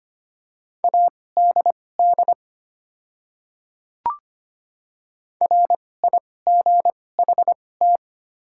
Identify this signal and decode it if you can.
Morse code, heard as a keyed tone, about 25 wpm, spaced as for 18 wpm, 700 Hz. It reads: ADD RIGHT
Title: 2x Character Spacing - Top 200 Words - 25wpm